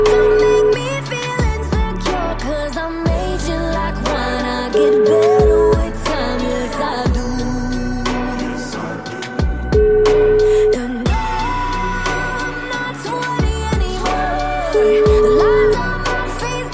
Ringback tone (Europe, 425 Hz), with noise.